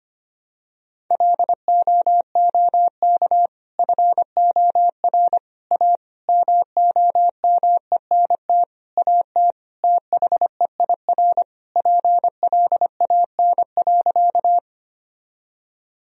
Morse code, heard as a keyed tone, about 25 wpm, 700 Hz. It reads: LOOK FOR A MOMENT AT THEIR PLAN.
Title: Sentences from Top 1000 Words - Review Part 2 - 25wpm